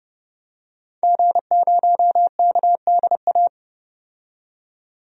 Morse code, heard as a keyed tone, about 30 wpm, 700 Hz. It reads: G0KDA
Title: DXCC-100 Call Signs - 5 Characters - 30wpm